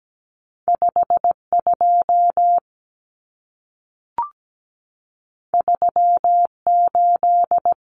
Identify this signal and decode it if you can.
Morse code, heard as a keyed tone, about 17 wpm, 700 Hz. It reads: 52 38
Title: Numbers - 2-Digit 17wpm